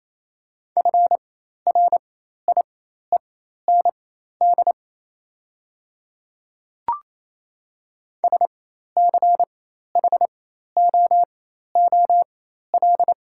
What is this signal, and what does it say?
Morse code, heard as a keyed tone, about 28 wpm, spaced as for 13 wpm, 700 Hz. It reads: FRIEND SCHOOL
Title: 4x Character Spacing - Top 300 Words - 28wpm